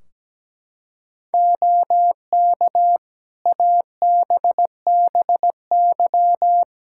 Morse code, heard as a keyed tone, about 17 wpm, 700 Hz. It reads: OK ABBY
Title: Practice QSO - Exchange 4 - Ending - 17wpm